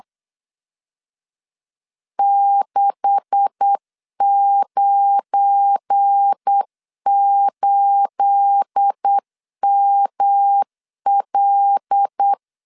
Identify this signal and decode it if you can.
Morse code, heard as a keyed tone, about 8 wpm, 800 Hz. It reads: 698ML
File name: LEZIONI / 9 286 8wpm 3425